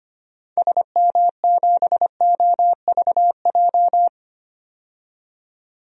Morse code, heard as a keyed tone, about 25 wpm, 700 Hz. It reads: SM7OVJ
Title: DXCC-100 Call Signs - 6 Characters - 25wpm